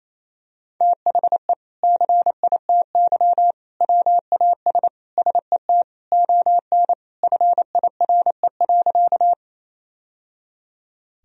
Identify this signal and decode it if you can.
Morse code, heard as a keyed tone, about 27 wpm, 700 Hz. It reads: THE CITY WAS SET ON FIRE.
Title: Sentences from Top 1000 Words - Review Part 1 - 28wpm